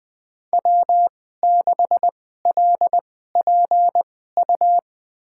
Morse code, heard as a keyed tone, about 20 wpm, spaced as for 15 wpm, 700 Hz. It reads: W6LPU
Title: USA Call Signs - 2x Character Spacing - 20wpm